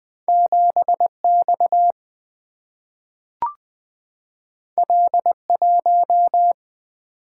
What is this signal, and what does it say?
Morse code, heard as a keyed tone, about 20 wpm, 700 Hz. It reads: 7X L1
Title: Single-Letter Single-Number 20wpm